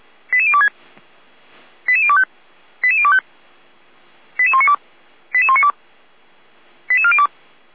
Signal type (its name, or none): five-tone selective call